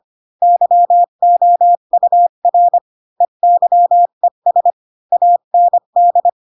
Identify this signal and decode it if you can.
Morse code, heard as a keyed tone, about 25 wpm, 700 Hz. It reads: YOUR EYES AND
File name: Pinocchio / Pinocchio ch8 20 25 ik1hge